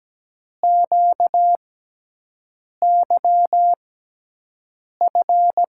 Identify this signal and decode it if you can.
Morse code, heard as a keyed tone, about 17 wpm, spaced as for 6 wpm, 700 Hz. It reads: QYF
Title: USA Call Signs - 6 Characters - 6x Character Spacing - 17wpm